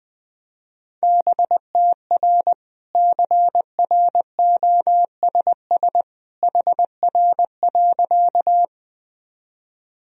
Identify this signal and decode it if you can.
Morse code, heard as a keyed tone, about 20 wpm, 700 Hz. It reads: BTR CROSS HR.
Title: Sentences from Top 400 Words and Abbreviations 20wpm